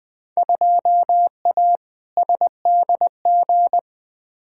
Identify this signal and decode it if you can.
Morse code, heard as a keyed tone, about 21 wpm, 700 Hz. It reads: 2A SDG
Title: Field Day Exchanges Encore - 20wpm